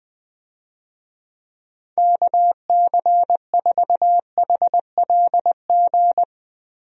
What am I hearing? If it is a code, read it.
Morse code, heard as a keyed tone, about 20 wpm, 700 Hz. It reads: KC4HLG
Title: USA Call Signs - 6 Characters - Repeated 3x - 20wpm